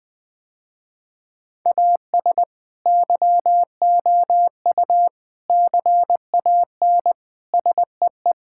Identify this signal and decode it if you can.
Morse code, heard as a keyed tone, about 20 wpm, 700 Hz. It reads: AS YOU CAN SEE
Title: Sentences with 6-Word Phrases - Spoken Beginning - 20wpm